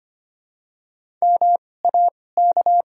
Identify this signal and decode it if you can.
Morse code, heard as a keyed tone, about 25 wpm, spaced as for 18 wpm, 700 Hz. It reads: MAK
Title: 2x Character Spacing - Top 300 Words - 25wpm